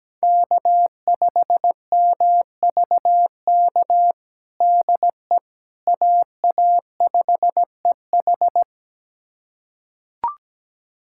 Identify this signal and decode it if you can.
Morse code, heard as a keyed tone, about 17 wpm, 700 Hz. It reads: K5MVK DE AA5EH
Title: USA Callsign DE Callsign - 5 Characters - 17wpm